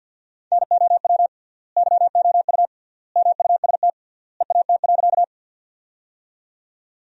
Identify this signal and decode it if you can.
Morse code, heard as a keyed tone, about 50 wpm, 700 Hz. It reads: NOW YOU MUST EAT.
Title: 3x Word-Spacing - Sentences from Top 300 Words - 50wpm